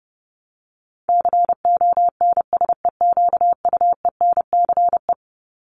Morse code, heard as a keyed tone, about 30 wpm, 700 Hz. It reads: CONSEQUENCE
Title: LY Suffix - Top 100 Words with Base Words 30wpm